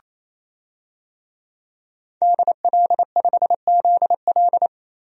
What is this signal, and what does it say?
Morse code, heard as a keyed tone, about 27 wpm, 700 Hz. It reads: DL5ZL
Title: DXCC-100 Call Signs - 5 Characters - Repeated 3x - 28wpm